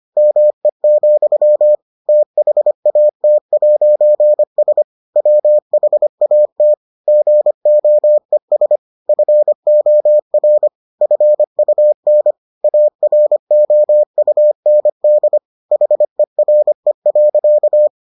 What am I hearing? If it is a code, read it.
Morse code, heard as a keyed tone, about 25 wpm, 600 Hz. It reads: ME, THAT'S WHAT GOES FOR FUN AROUND HERE.